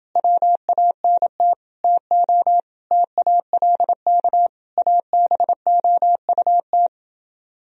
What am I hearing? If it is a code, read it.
Morse code, heard as a keyed tone, about 30 wpm, 700 Hz. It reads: WANT TO TALK ABOUT